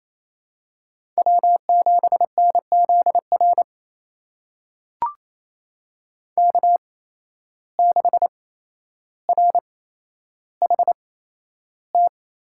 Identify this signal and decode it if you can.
Morse code, heard as a keyed tone, about 28 wpm, spaced as for 12 wpm, 700 Hz. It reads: W7NZR K6RHT
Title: USA Call Signs - 8x Character Spacing - 28wpm